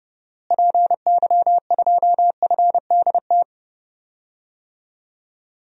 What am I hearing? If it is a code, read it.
Morse code, heard as a keyed tone, about 32 wpm, 700 Hz. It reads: PY2FDT
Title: DXCC-100 Call Signs - 6 Characters - Repeated 2x - 30wpm